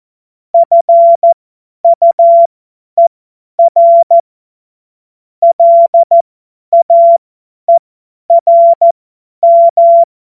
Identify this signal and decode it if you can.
Morse code, heard as a keyed tone, about 14 wpm, spaced as for 10 wpm, 680 Hz. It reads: FUER LAERM
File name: Morsekurs / Witze / Witz 25. Juli 2023 50 F70